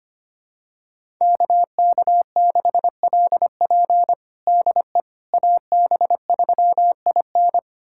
Morse code, heard as a keyed tone, about 25 wpm, 700 Hz. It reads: KK6LP DE AB3IN
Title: USA Callsign DE Callsign - 5 Characters - 25wpm